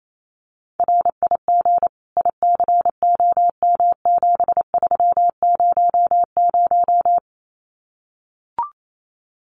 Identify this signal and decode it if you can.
Morse code, heard as a keyed tone, about 28 wpm, 700 Hz. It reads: RIG ICOM7300
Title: Common QSO Phrases 28wpm